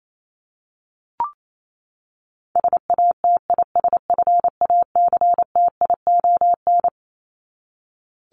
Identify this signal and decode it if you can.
Morse code, heard as a keyed tone, about 28 wpm, 700 Hz. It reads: SATISFACTION
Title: Top 1751-2000 Words 28wpm